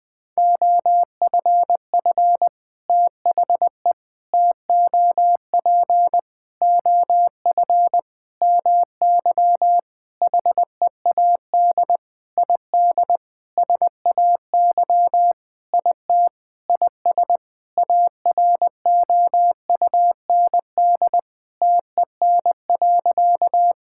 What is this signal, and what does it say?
Morse code, heard as a keyed tone, about 20 wpm, 700 Hz. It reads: OFF THE TOP OF MY HEAD ID SAY IT IS AROUND TEN.
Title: Sentences with 6-Word Phrases - 20wpm